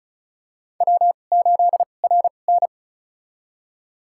Morse code, heard as a keyed tone, about 36 wpm, spaced as for 25 wpm, 700 Hz. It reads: W8RN
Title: USA Call Signs - 4 Characters - 2x Character Spacing - 35wpm